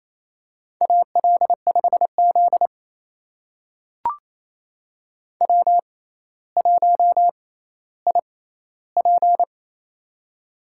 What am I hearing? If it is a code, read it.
Morse code, heard as a keyed tone, about 28 wpm, spaced as for 14 wpm, 700 Hz. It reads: AL5Z W1IP
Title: USA Call Signs - 4 Characters - 6x Character Spacing - 28wpm